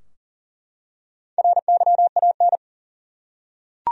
Morse code, heard as a keyed tone, about 40 wpm, 700 Hz. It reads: RYAN
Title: USA Common Names - Max 4 Letters - 40wpm